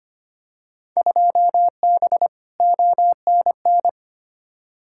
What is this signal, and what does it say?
Morse code, heard as a keyed tone, about 25 wpm, 700 Hz. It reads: 2B ONN
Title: Field Day Exchanges - 25wpm